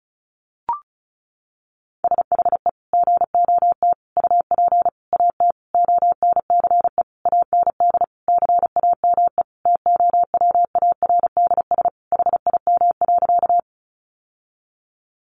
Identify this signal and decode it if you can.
Morse code, heard as a keyed tone, about 34 wpm, 700 Hz. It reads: SHE GOT UP AT ONCE AND CAME TOWARDS HIM.
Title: Sentences from Top 1100 Words 35wpm